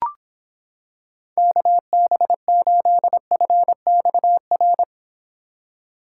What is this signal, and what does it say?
Morse code, heard as a keyed tone, about 26 wpm, 700 Hz. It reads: KB8FXR